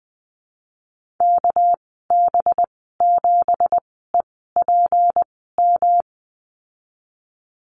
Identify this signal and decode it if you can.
Morse code, heard as a keyed tone, about 20 wpm, spaced as for 15 wpm, 700 Hz. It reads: KB7EPM
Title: 200 Lesson 32g - Letter k - Call Signs 20wpm